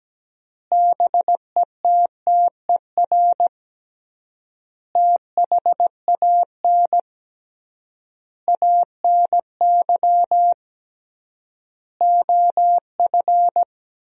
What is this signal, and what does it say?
Morse code, heard as a keyed tone, about 17 wpm, 700 Hz. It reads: BETTER THAN ANY OF
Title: 3x Word-Spacing - Sentences from Top 400 Words - 17wpm